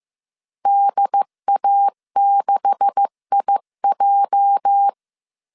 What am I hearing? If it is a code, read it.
Morse code, heard as a keyed tone, about 15 wpm, 800 Hz. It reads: DA6IJ
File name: CW / DET / 1 9 1500C S B 14wpm 3302